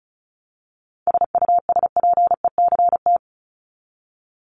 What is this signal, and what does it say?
Morse code, heard as a keyed tone, about 34 wpm, 700 Hz. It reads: SUSPECT
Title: Top 1751-2000 Words 35wpm